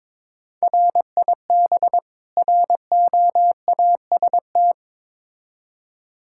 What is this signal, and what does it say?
Morse code, heard as a keyed tone, about 21 wpm, 700 Hz. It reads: RIB ROAST
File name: Difficult Words with Hints - 22wpm